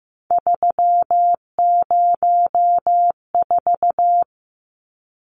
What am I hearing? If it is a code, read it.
Morse code, heard as a keyed tone, about 15 wpm, 700 Hz. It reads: 304
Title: Numbers - 3-Digit 15wpm